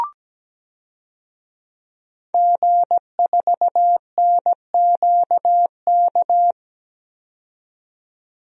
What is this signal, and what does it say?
Morse code, heard as a keyed tone, about 17 wpm, 700 Hz. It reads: G4NQK
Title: DXCC-100 Call Signs - 5 Characters - 17wpm